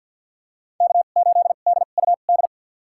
Morse code, heard as a keyed tone, about 50 wpm, spaced as for 36 wpm, 700 Hz. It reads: K8DUD
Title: USA Call Signs - 5 Characters - 2x Character Spacing - 50wpm